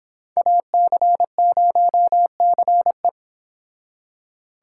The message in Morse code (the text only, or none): AC0CE